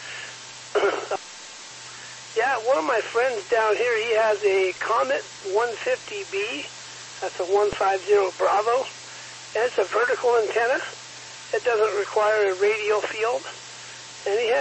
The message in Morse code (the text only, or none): none